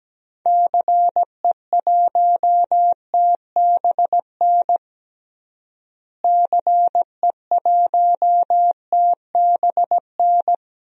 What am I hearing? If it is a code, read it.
Morse code, heard as a keyed tone, about 17 wpm, 700 Hz. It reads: CE1TBN CE1TBN
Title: DXCC-100 Call Signs - 6 Characters - Repeated 3x - 17wpm